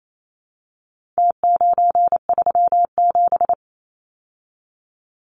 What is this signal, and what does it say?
Morse code, heard as a keyed tone, about 28 wpm, 700 Hz. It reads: T937
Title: Numbers - Contest Style - 4-Digit 28wpm